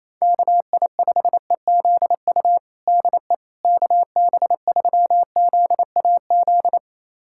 Morse code, heard as a keyed tone, about 27 wpm, 700 Hz. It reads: KI5EZU DE KB3ZAZ
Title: USA Callsign DE Callsign - 6 Characters - 28wpm